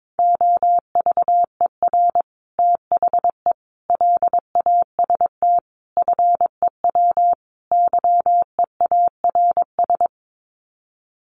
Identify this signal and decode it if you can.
Morse code, heard as a keyed tone, about 22 wpm, 700 Hz. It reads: OVER THE LAST FEW YEARS
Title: Sets of 5 Words Encore 22wpm